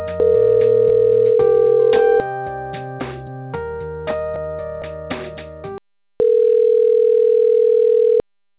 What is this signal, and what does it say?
Ringback tone (North America), 440 + 480 Hz.